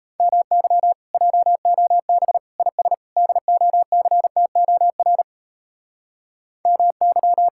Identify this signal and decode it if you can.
Morse code, heard as a keyed tone, about 36 wpm, 700 Hz. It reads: MY JOB IS DOCTOR MY